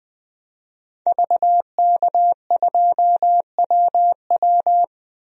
Morse code, heard as a keyed tone, about 21 wpm, 700 Hz. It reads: VK2WW
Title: DXCC-100 Call Signs - 5 Characters - 20wpm